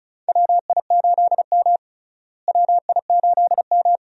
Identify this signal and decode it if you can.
Morse code, heard as a keyed tone, about 36 wpm, 700 Hz. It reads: WI8M WI8M
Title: USA Call Signs - 4 Characters - Repeated 2x - 35wpm